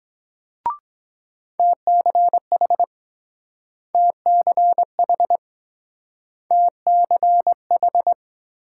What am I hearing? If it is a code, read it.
Morse code, heard as a keyed tone, about 23 wpm, 700 Hz. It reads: TCH TCH TCH